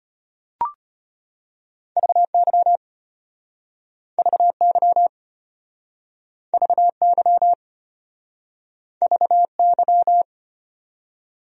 Morse code, heard as a keyed tone, about 34 wpm, 700 Hz. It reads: VY VY VY VY